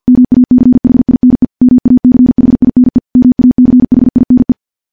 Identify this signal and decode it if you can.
Morse code, heard as a keyed tone, about 50 wpm, 250 Hz. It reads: MACHINE MACHINE MACHINE
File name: morse / morse-training-50WPM